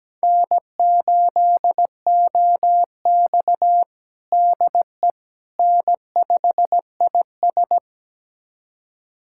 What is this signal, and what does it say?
Morse code, heard as a keyed tone, about 17 wpm, 700 Hz. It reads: N8OX DE N5IS
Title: USA Callsign DE Callsign - 4 Characters - 17wpm